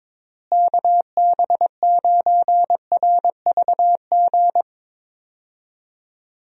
Morse code, heard as a keyed tone, about 22 wpm, 700 Hz. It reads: KB9RVG
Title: USA Call Signs - 6 Characters - Repeated 2x - 22wpm